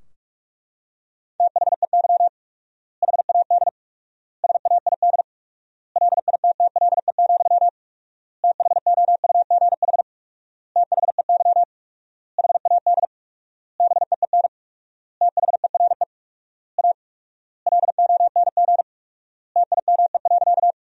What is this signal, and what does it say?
Morse code, heard as a keyed tone, about 41 wpm, 700 Hz. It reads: THEY HAD SAID LITTLE, THOUGH THEY HAD BEEN THERE A LONG TIME.
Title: 4x Word-Spacing - Sentences from Top 400 Words - 45wpm